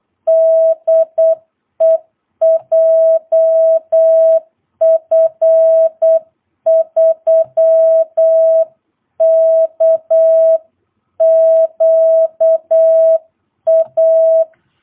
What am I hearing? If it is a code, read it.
Morse code, heard as a keyed tone, about 8 wpm, 660 Hz. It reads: DE JF3KQA
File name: jh8cbh / 2022cw